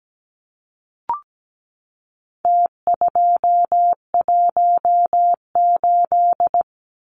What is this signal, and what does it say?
Morse code, heard as a keyed tone, about 17 wpm, 700 Hz. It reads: T218
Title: Numbers - Contest Style - 4-Digit 17wpm